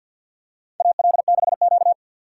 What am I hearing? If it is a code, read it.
Morse code, heard as a keyed tone, about 52 wpm, 700 Hz. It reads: AL6Q
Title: USA Call Signs - 4 Characters - 2x Character Spacing - 50wpm